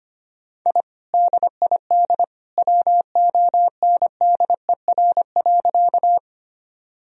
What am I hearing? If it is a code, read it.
Morse code, heard as a keyed tone, about 25 wpm, 700 Hz. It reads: I DID WONDER.
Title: Sentences from Top 1000 Words - Review Part 3 - 25wpm